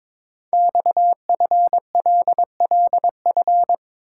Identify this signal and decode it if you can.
Morse code, heard as a keyed tone, about 23 wpm, 700 Hz. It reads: XFLLF
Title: Code Groups - 5 Letters - 22wpm